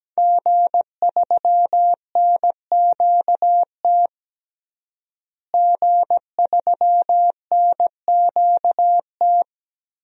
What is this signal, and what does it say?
Morse code, heard as a keyed tone, about 17 wpm, 700 Hz. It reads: G3NQT G3NQT
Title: DXCC-100 Call Signs - 5 Characters - Repeated 2x - 17wpm